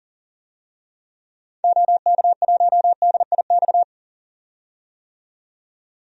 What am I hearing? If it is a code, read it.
Morse code, heard as a keyed tone, about 40 wpm, 700 Hz. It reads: OK1DIX
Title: DXCC-100 Call Signs - 6 Characters - 40wpm